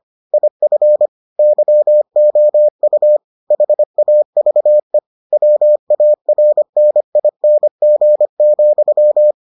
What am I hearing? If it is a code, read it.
Morse code, heard as a keyed tone, about 25 wpm, 600 Hz. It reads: IF YOU HAVE WARNING,